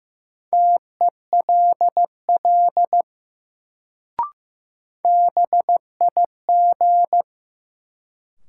Morse code, heard as a keyed tone, about 15 wpm, 700 Hz. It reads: TELL BIG
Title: Sentences from Top 200 Words - New Word Drill - 15wpm